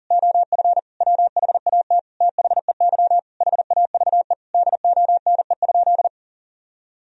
Morse code, heard as a keyed tone, about 40 wpm, 700 Hz. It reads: OF WHAT THEY HAVE DONE?
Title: Sentences from Top 1000 Words - Review Part 3 - 40wpm